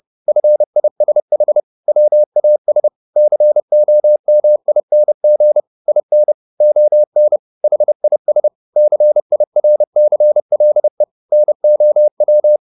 Morse code, heard as a keyed tone, about 30 wpm, 600 Hz. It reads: FISH WAS COMING IN ON HIS CIRCLE NOW